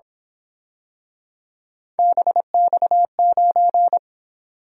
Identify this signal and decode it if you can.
Morse code, heard as a keyed tone, about 27 wpm, 700 Hz. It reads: BX9
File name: Speed-Racing - Three Letter-Number Combinations - 22wpm